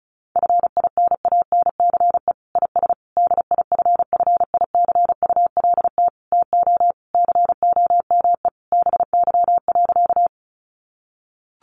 Morse code, heard as a keyed tone, about 34 wpm, 700 Hz. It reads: FINANCE IS DIFFICULT TO COME BY.
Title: Sentences from Top 1500 Words 35wpm